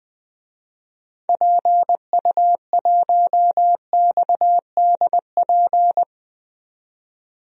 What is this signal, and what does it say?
Morse code, heard as a keyed tone, about 20 wpm, 700 Hz. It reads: PU1XDP
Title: DXCC-100 Call Signs - 6 Characters - 20wpm